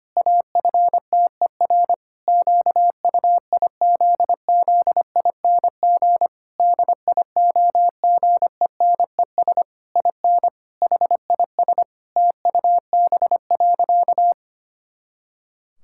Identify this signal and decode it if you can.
Morse code, heard as a keyed tone, about 25 wpm, 700 Hz. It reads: AFTER QUIZZING DIOGENES IN HIS TUB.
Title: Pangram Challenge v2 - 25wpm